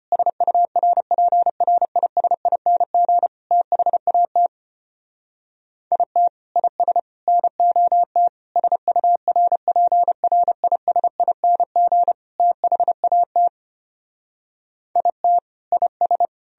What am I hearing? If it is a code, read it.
Morse code, heard as a keyed tone, about 33 wpm, 700 Hz. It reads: SURPRISING THAT IT IS NOT SURPRISING THAT IT IS